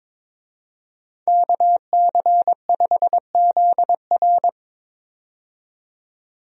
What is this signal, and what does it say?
Morse code, heard as a keyed tone, about 22 wpm, 700 Hz. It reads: KC5ZR
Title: USA Call Signs - 5 Characters - 22wpm